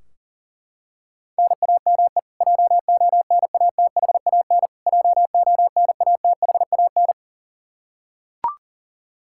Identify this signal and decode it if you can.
Morse code, heard as a keyed tone, about 40 wpm, 700 Hz. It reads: NAME JONATHAN JONATHAN
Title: Practice QSO - Exchange 1 - Beginning - 40wpm